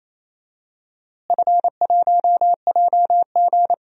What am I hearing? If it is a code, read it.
Morse code, heard as a keyed tone, about 28 wpm, 700 Hz. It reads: F1JG
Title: DXCC-100 Call Signs - 4 Characters - Repeated 3x - 28wpm